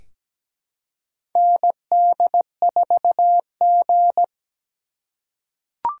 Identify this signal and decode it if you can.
Morse code, heard as a keyed tone, about 17 wpm, 700 Hz. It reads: ND4G
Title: K1USN SST QSO Practice - 17wpm